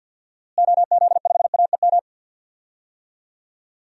Morse code, heard as a keyed tone, about 48 wpm, 700 Hz. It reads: OZ5AEM
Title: DXCC-100 Call Signs - 6 Characters - Repeated 3x - 50wpm